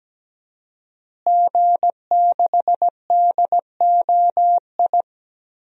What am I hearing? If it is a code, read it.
Morse code, heard as a keyed tone, about 17 wpm, 700 Hz. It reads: G6DOI
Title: DXCC-100 Call Signs - 5 Characters - Repeated 3x - 17wpm